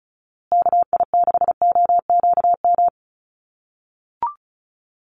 Morse code, heard as a keyed tone, about 34 wpm, 700 Hz. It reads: KI6OQM
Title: USA Call Signs Repeated 3x 35wpm